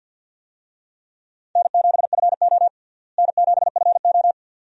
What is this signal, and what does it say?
Morse code, heard as a keyed tone, about 49 wpm, 700 Hz. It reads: N7FO N7FO
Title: USA Call Signs - 4 Characters - Repeated 2x - 50wpm